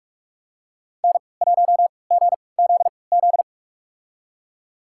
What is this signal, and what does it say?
Morse code, heard as a keyed tone, about 48 wpm, spaced as for 26 wpm, 700 Hz. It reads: N1GZZ
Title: USA Call Signs - 5 Characters - 3x Character Spacing - 45wpm